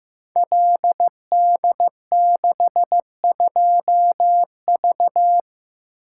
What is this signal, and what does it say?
Morse code, heard as a keyed tone, about 15 wpm, 700 Hz. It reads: LD62V
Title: Code Groups - 5 Letter:Numbers - 15wpm